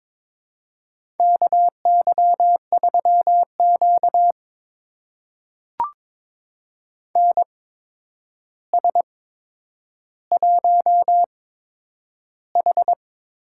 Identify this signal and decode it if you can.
Morse code, heard as a keyed tone, about 22 wpm, spaced as for 9 wpm, 700 Hz. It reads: KY3Q NS1H
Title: USA Call Signs - 4 Characters - 8x Character Spacing - 22wpm